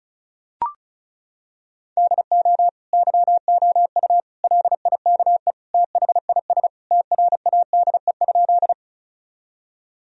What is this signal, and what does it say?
Morse code, heard as a keyed tone, about 34 wpm, 700 Hz. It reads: DO YOU LIKE THIS TRADE?
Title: Sentences from Top 1000 Words - Review Part 4 - 35wpm